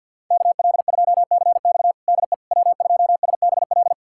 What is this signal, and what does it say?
Morse code, heard as a keyed tone, about 52 wpm, 700 Hz. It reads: KL2CX DE W2SBL
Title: USA Callsign DE Callsign - 5 Characters - 50wpm